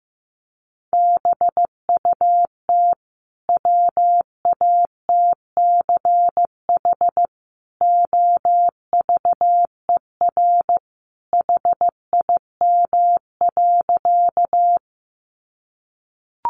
Morse code, heard as a keyed tone, about 15 wpm, 700 Hz. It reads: BUT WATCH OVER HIM.
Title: Sentences from Top 300 Words 15wpm